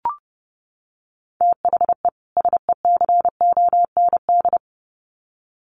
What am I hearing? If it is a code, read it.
Morse code, heard as a keyed tone, about 30 wpm, 700 Hz. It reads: THE SECOND
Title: Sets of 2 Words 30wpm